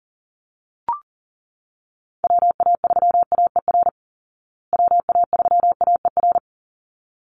Morse code, heard as a keyed tone, about 40 wpm, 700 Hz. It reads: WA3AER WA3AER
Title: Call Signs Repeated 3x - 40wpm